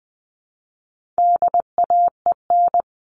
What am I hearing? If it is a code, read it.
Morse code, heard as a keyed tone, about 20 wpm, 700 Hz. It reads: DAEN
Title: All Cut Numbers - 4-Digit 20wpm